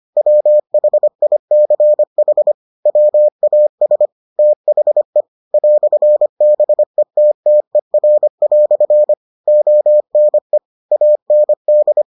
Morse code, heard as a keyed tone, about 25 wpm, 600 Hz. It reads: WHICH WAS THE "BETTER" ONE AND